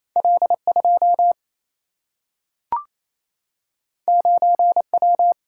Morse code, heard as a keyed tone, about 28 wpm, 700 Hz. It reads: L2 9W
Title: Single-Letter Single-Number 28wpm